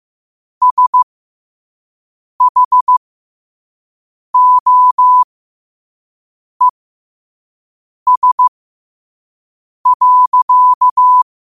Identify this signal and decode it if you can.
Morse code, heard as a keyed tone, about 15 wpm, spaced as for 5 wpm, 1000 Hz. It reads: SHOES.
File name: qotdfiles / 5wpm